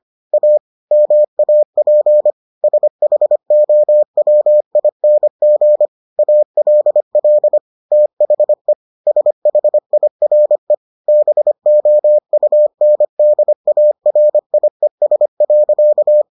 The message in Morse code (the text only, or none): A MAP SHOWING ALL THE SHIRE BOUNDARIES.